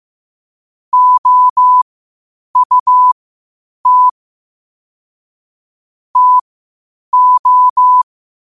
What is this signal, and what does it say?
Morse code, heard as a keyed tone, about 15 wpm, spaced as for 8 wpm, 1000 Hz. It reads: OUT TO
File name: qotdfiles / 8wpm